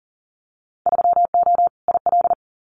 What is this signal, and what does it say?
Morse code, heard as a keyed tone, about 40 wpm, 700 Hz. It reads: 3O IL
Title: Winter Field Day Exchanges 40wpm